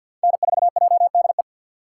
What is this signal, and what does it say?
Morse code, heard as a keyed tone, about 54 wpm, 700 Hz. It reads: N4JDE
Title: USA Call Signs - 5 Characters - 6x Character Spacing - 50wpm